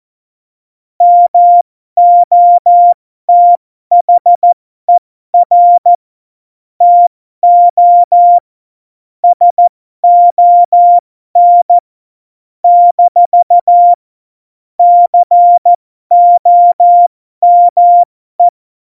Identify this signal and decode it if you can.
Morse code, heard as a keyed tone, about 14 wpm, spaced as for 12 wpm, 700 Hz. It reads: MOTHER TO SON - COME
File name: Morsekurs / Witze